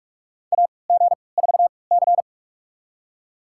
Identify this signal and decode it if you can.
Morse code, heard as a keyed tone, about 47 wpm, spaced as for 26 wpm, 700 Hz. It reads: AG4C
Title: USA Call Signs - 4 Characters - 3x Character Spacing - 45wpm